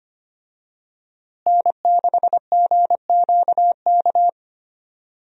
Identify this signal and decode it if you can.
Morse code, heard as a keyed tone, about 25 wpm, 700 Hz. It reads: N6GQK
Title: USA Call Signs - 5 Characters - Repeated 2x - 25wpm